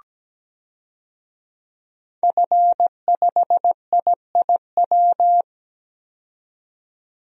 Morse code, heard as a keyed tone, about 17 wpm, 700 Hz. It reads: F5IIW
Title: DXCC-100 Call Signs - 5 Characters - 17wpm